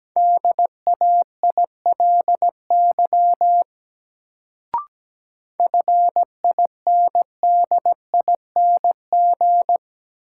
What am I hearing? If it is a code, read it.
Morse code, heard as a keyed tone, about 17 wpm, 700 Hz. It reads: DAILY FINDING
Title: Sentences from Top 1100 Words - New Word Drill - 17wpm